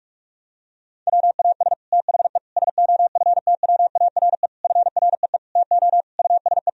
Morse code, heard as a keyed tone, about 41 wpm, 700 Hz. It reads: WAS THE SOFTWARE FREE TO USE
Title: Sentences from Top 1000 Words - Review Part 3 - 45wpm